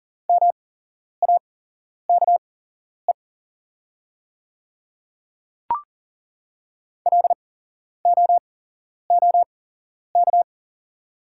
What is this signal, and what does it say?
Morse code, heard as a keyed tone, about 40 wpm, spaced as for 11 wpm, 700 Hz. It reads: MAKE LOOK
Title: 8x Character Spacing - Top 100 Words - 40wpm